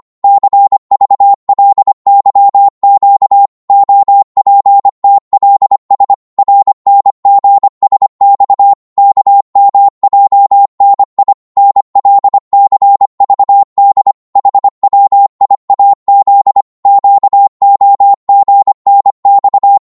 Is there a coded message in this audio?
Yes: Morse code from 0.2 to 19.8 s, heard as a keyed tone, about 25 wpm, 800 Hz. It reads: CVLYQ OPTLS RNGSX KMJNI NLCVD HWIAZ QOGNX